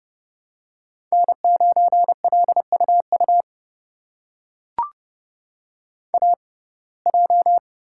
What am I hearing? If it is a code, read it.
Morse code, heard as a keyed tone, about 30 wpm, spaced as for 22 wpm, 700 Hz. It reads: N9LUU AJ
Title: USA Call Signs - 5 Characters - 6x Character Spacing - 30wpm